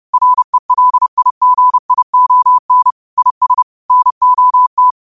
Morse code, heard as a keyed tone, about 30 wpm, 1000 Hz. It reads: RELIGION IS NOT
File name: qotdfiles / 30wpm